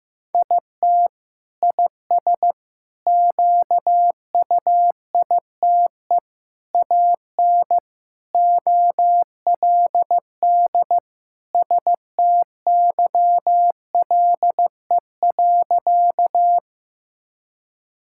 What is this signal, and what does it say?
Morse code, heard as a keyed tone, about 15 wpm, 700 Hz. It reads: IT IS QUITE AN OLD STYLE.
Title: Sentences from Top 1000 Words - Review Part 5 - 15wpm